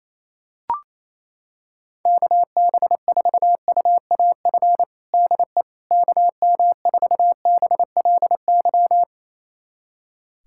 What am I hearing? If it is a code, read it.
Morse code, heard as a keyed tone, about 28 wpm, 700 Hz. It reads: KB4UAF DE KM4BLY
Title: USA Callsign DE Callsign 28wpm